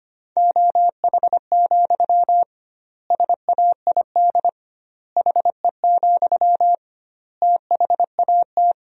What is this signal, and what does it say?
Morse code, heard as a keyed tone, about 25 wpm, 700 Hz. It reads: OH, SAID HE, THAT
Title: 2x Word-Spacing - Sentences from Top 500 Words - 25wpm